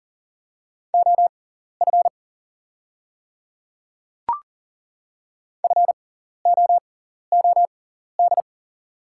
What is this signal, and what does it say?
Morse code, heard as a keyed tone, about 40 wpm, spaced as for 14 wpm, 700 Hz. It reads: OF FOOD
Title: 6x Character Spacing - Top 300 Words - 40wpm